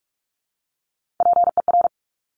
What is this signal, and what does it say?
Morse code, heard as a keyed tone, about 46 wpm, 700 Hz. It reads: PER
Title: Top Three Letter Combinations in Words 45wpm